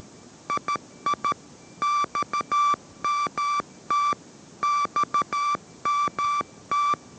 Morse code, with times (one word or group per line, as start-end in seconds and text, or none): 0.5-1.3 II
1.8-4.1 XMT
4.6-6.9 XMT